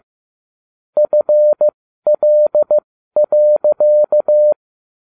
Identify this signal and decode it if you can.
Morse code, heard as a keyed tone, about 15 wpm, spaced as for 12 wpm, 600 Hz. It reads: FL.